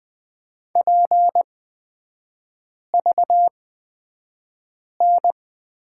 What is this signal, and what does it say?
Morse code, heard as a keyed tone, about 20 wpm, spaced as for 5 wpm, 700 Hz. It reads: PVN
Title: 243 Lesson 39e - Letter z - Three Characters 20wpm